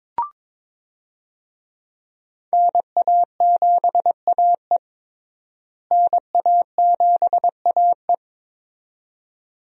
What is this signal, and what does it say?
Morse code, heard as a keyed tone, about 22 wpm, 700 Hz. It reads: NA7AE NA7AE
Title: USA Call Signs - 5 Characters - Repeated 2x - 22wpm